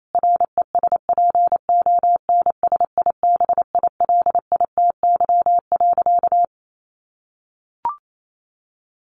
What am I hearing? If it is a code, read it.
Morse code, heard as a keyed tone, about 28 wpm, 700 Hz. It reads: RESPONSIBILITY.
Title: Sentences from Top 1500 Words 28wpm